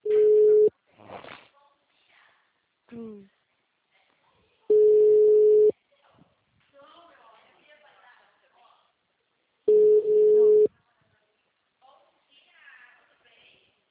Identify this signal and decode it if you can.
Ringback tone (Europe, 425 Hz), with speech.